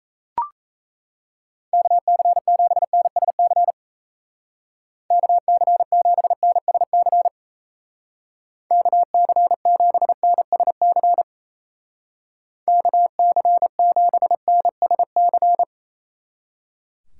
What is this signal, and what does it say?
Morse code, heard as a keyed tone, about 36 wpm, 700 Hz. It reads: KC7NSC KC7NSC KC7NSC KC7NSC